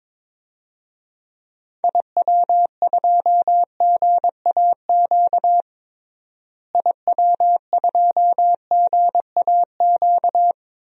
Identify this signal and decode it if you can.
Morse code, heard as a keyed tone, about 22 wpm, 700 Hz. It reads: IW2GAQ IW2GAQ
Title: DXCC-100 Call Signs - 6 Characters - Repeated 3x - 22wpm